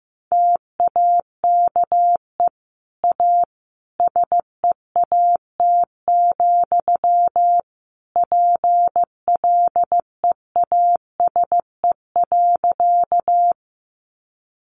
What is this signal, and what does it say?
Morse code, heard as a keyed tone, about 15 wpm, 700 Hz. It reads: TAKE A SEAT, PLEASE.
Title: Sentences from Top 1500 Words 15wpm